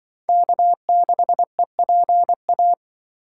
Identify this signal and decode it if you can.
Morse code, heard as a keyed tone, about 25 wpm, 700 Hz. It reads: K6EPA